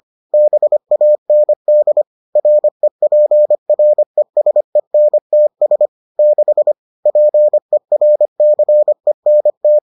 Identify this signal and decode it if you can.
Morse code, heard as a keyed tone, about 25 wpm, 600 Hz. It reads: BAND REPRESENTS 6 PERCENT